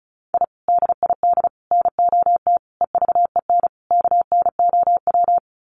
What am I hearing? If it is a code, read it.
Morse code, heard as a keyed tone, about 34 wpm, 700 Hz. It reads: I DID NOT EVEN KNOW
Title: Sets of 5 Words Encore 35wpm